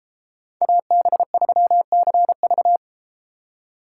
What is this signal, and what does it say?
Morse code, heard as a keyed tone, about 33 wpm, 700 Hz. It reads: AB3CV